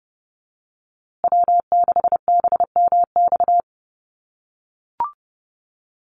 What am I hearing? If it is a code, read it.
Morse code, heard as a keyed tone, about 30 wpm, 700 Hz. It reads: W6BMX
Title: USA Call Signs Repeated 3x 30wpm